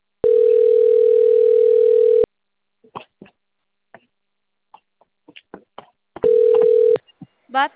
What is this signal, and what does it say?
Ringback tone (North America), 440 + 480 Hz.
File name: out- - - - - .60531